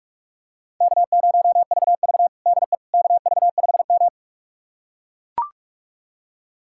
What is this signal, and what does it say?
Morse code, heard as a keyed tone, about 42 wpm, 700 Hz. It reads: K0VV DE KV5M
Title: USA Callsign DE Callsign - 4 Characters - 45wpm